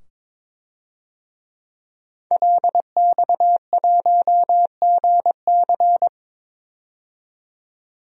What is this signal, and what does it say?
Morse code, heard as a keyed tone, about 22 wpm, 700 Hz. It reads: LX1GC
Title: DXCC-100 Call Signs - 5 Characters - 22wpm